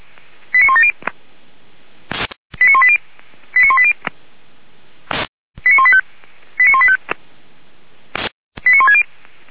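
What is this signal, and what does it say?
Five-tone selective call (ZVEI), heard as quick runs of five tones, tones of about 70 ms. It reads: 89170 89180 89180 89185 89185 89160
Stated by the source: ZVEI five-tone call received by SDR, from the server of feuerwehr-orlamuende.org.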